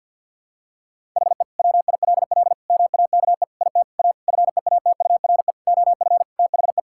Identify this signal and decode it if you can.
Morse code, heard as a keyed tone, about 50 wpm, 700 Hz. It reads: HE WILL MAKE IT A FEATURE OF THE
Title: Sentences from Top 1000 Words - Review Part 5 - 50wpm